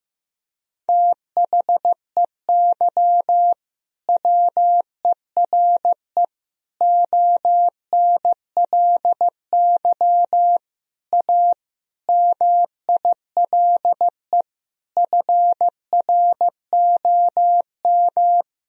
Morse code, heard as a keyed tone, about 15 wpm, 700 Hz. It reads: THEY WERE ONLY A MILE FROM
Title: Sentences from Top 1000 Words - Review Part 1 - 15wpm